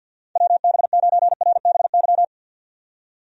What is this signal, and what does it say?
Morse code, heard as a keyed tone, about 51 wpm, 700 Hz. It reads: WB9RBY
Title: USA Call Signs - 6 Characters - 50wpm